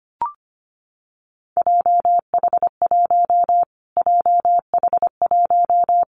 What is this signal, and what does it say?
Morse code, heard as a keyed tone, about 25 wpm, 700 Hz. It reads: JH1 JH1
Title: Callsign Prefixes for Top 25 DXCC - Repeated 3x - 25wpm